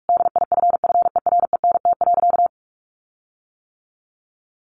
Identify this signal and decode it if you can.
Morse code, heard as a keyed tone, about 44 wpm, 700 Hz. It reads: DIFFERENT.
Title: Sentences from Top 1300 Words 45wpm